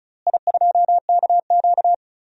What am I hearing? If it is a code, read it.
Morse code, heard as a keyed tone, about 34 wpm, 700 Hz. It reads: I2KQ
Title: DXCC-100 Call Signs - 4 Characters - Repeated 3x - 35wpm